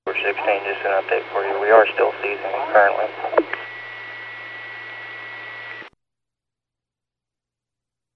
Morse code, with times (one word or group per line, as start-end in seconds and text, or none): none